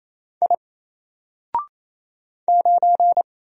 Morse code, heard as a keyed tone, about 27 wpm, 700 Hz. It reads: I 9
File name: Single-Letter-Number-28wpm